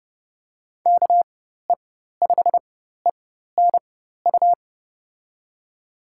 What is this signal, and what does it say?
Morse code, heard as a keyed tone, about 30 wpm, spaced as for 14 wpm, 700 Hz. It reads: KE5ENU
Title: USA Call Signs - 6 Characters - 4x Character Spacing - 30wpm